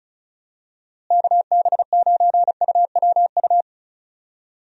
Morse code, heard as a keyed tone, about 36 wpm, 700 Hz. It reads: KB9UWU
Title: USA Call Signs - 6 Characters - Repeated 2x - 35wpm